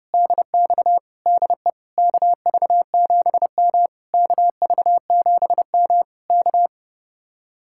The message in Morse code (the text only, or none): DX DE KV7M KV7M K